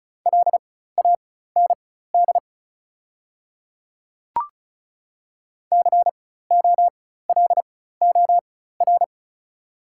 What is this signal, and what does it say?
Morse code, heard as a keyed tone, about 34 wpm, spaced as for 16 wpm, 690 Hz. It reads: LAND COLOR
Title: 4x Character Spacing - Top 300 Words - 35wpm